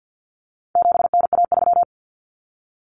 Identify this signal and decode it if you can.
Morse code, heard as a keyed tone, about 50 wpm, 700 Hz. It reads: 7NA3
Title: Common Cut Numbers - 4-Digit 50wpm